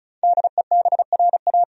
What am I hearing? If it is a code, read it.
Morse code, heard as a keyed tone, about 36 wpm, 700 Hz. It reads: DEBRA
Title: USA Common Names - Max 5 Letters - 35wpm